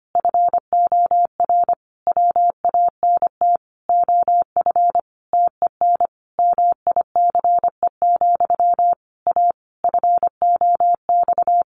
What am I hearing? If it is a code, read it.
Morse code, heard as a keyed tone, about 25 wpm, 700 Hz. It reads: FOR WANT OF TEN MICE, A FOX
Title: A Sound Of Thunder 25wpm